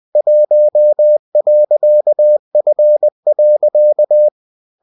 Morse code, heard as a keyed tone, about 20 wpm, 600 Hz. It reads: 1.F.